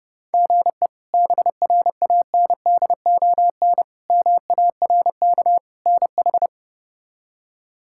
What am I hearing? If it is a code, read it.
Morse code, heard as a keyed tone, about 32 wpm, 700 Hz. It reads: GE BRANDON MARK NH
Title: K1USN SST QSO Practice - 30wpm